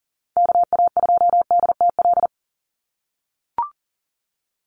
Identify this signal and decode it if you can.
Morse code, heard as a keyed tone, about 40 wpm, 700 Hz. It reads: KA2DTL
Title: USA Call Signs 40wpm